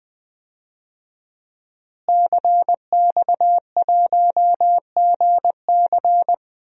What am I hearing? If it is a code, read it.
Morse code, heard as a keyed tone, about 21 wpm, 700 Hz. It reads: CX1GC
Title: DXCC-100 Call Signs - 5 Characters - Repeated 3x - 20wpm